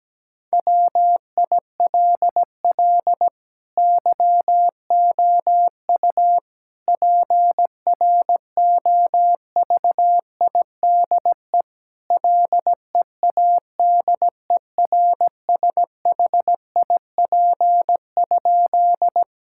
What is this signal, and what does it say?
Morse code, heard as a keyed tone, about 17 wpm, 700 Hz. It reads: WILL YOU PROVIDE LEADERSHIP?
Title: Sentences from Top 1600 Words 17wpm